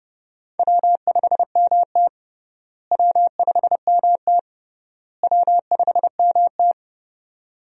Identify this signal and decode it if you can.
Morse code, heard as a keyed tone, about 30 wpm, 700 Hz. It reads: W5MT W5MT W5MT
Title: USA Call Signs - 4 Characters - Repeated 3x - 30wpm